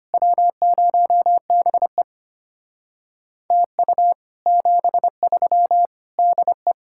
Morse code, heard as a keyed tone, about 32 wpm, 700 Hz. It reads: W0BE TU 73 DE